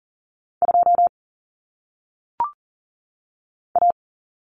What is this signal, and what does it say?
Morse code, heard as a keyed tone, about 41 wpm, 700 Hz. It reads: 2 A
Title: Common Cut Numbers - 1-Digit 40wpm